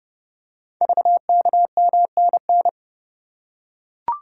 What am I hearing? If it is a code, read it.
Morse code, heard as a keyed tone, about 30 wpm, 700 Hz. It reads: VKMNN
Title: Code Groups - 5 Letters - 30wpm